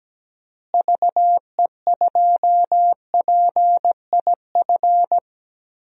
Morse code, heard as a keyed tone, about 17 wpm, 700 Hz. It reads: VE2PIF
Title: DXCC-100 Call Signs - 6 Characters - Repeated 2x - 17wpm